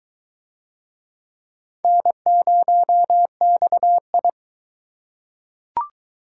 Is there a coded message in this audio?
Yes: Morse code from 1.8 to 4.3 s, heard as a keyed tone, about 22 wpm, 700 Hz. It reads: N0XI